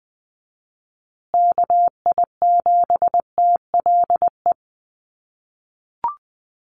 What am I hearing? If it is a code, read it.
Morse code, heard as a keyed tone, about 20 wpm, 700 Hz. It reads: KI7TLE
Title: USA Call Signs Repeated 2x 20wpm